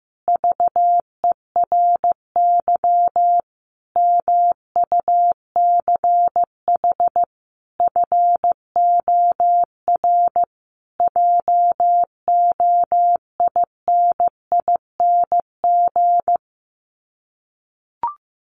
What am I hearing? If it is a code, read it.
Morse code, heard as a keyed tone, about 15 wpm, 700 Hz. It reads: VERY MUCH FOR JOINING
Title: Sets of 5 Words Encore2 15wpm